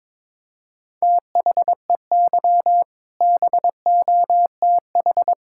Morse code, heard as a keyed tone, about 22 wpm, 700 Hz. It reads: THEY BOTH
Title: Sentences from Top 1600 Words 22wpm